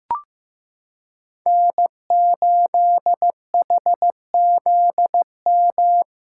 Morse code, heard as a keyed tone, about 15 wpm, 700 Hz. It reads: N8HZM
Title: K1USN SST QSO Practice - 15wpm